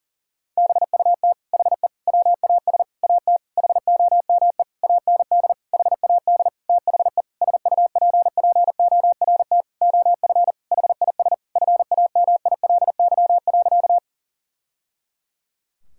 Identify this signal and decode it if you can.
Morse code, heard as a keyed tone, about 40 wpm, 700 Hz. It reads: BUT HE WAS AT HOME AND HAD THE SUPPORT OF HIS FAMILY.
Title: Sentences from Top 1000 Words - Review Part 5 - 40wpm